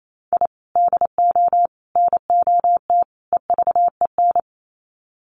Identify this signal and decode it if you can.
Morse code, heard as a keyed tone, about 28 wpm, 700 Hz. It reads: I DO NOT EVEN
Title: Sets of 4 Words 28wpm